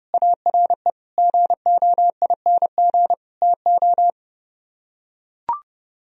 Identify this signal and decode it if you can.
Morse code, heard as a keyed tone, about 30 wpm, 700 Hz. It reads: ARE GOING TO